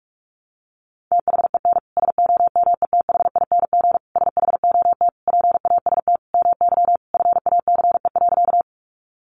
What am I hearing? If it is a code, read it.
Morse code, heard as a keyed tone, about 44 wpm, 700 Hz. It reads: THEN SOMETHING SHOT PAST MY FACE.
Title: Sentences from Top 1100 Words 45wpm